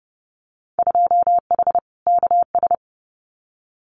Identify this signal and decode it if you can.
Morse code, heard as a keyed tone, about 30 wpm, 700 Hz. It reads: 2H KS
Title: Winter Field Day Exchanges Encore 30wpm